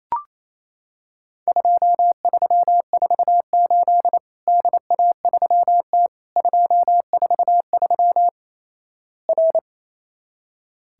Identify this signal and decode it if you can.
Morse code, heard as a keyed tone, about 28 wpm, 700 Hz. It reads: 2348 DA3T 243 R
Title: Worked All Europe DX Contest - QTC Exchange - 28wpm